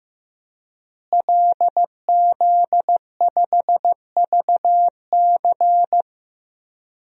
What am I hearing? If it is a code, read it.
Morse code, heard as a keyed tone, about 15 wpm, 700 Hz. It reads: LZ5VC
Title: DXCC-100 Call Signs - 5 Characters - Repeated 3x - 15wpm